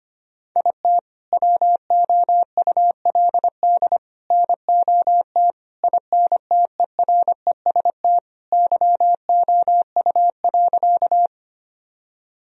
Morse code, heard as a keyed tone, about 25 wpm, 700 Hz. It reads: IT WOULD NOT INTEREST YOU.
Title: Sentences from Top 1000 Words - Review Part 2 - 25wpm